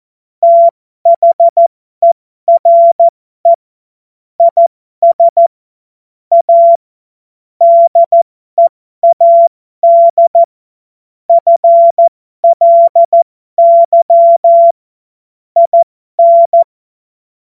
Morse code, heard as a keyed tone, about 14 wpm, spaced as for 12 wpm, 680 Hz. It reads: THERE IS A DEAD FLY IN
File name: Morsekurs / Witze